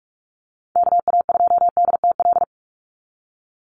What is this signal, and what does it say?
Morse code, heard as a keyed tone, about 44 wpm, 700 Hz. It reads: KA2DTL
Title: USA Call Signs 45wpm